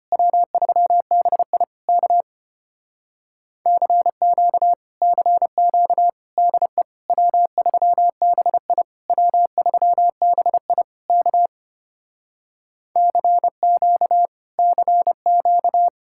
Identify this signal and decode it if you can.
Morse code, heard as a keyed tone, about 30 wpm, 700 Hz. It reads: W3BI K CQ CQ DE W3BI W3BI K CQ CQ
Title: Speed-Racing Common QSO Phrases Encore - 25wpm